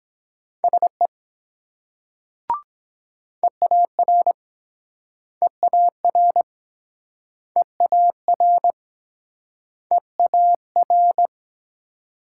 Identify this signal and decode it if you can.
Morse code, heard as a keyed tone, about 24 wpm, 700 Hz. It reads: SE EAR EAR EAR EAR